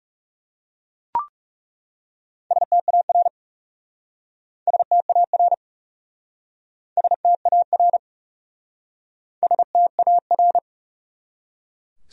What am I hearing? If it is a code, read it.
Morse code, heard as a keyed tone, about 37 wpm, 700 Hz. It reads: STAR STAR STAR STAR
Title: Speed-Racing Top 500 Words 30wpm